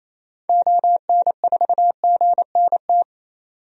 Morse code, heard as a keyed tone, about 28 wpm, 700 Hz. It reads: ON4GNT
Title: DXCC-100 Call Signs - 6 Characters - 28wpm